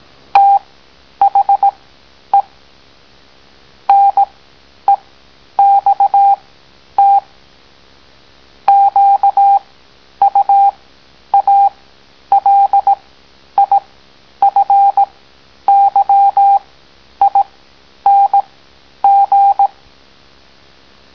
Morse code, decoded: THE NEXT QUALIFYING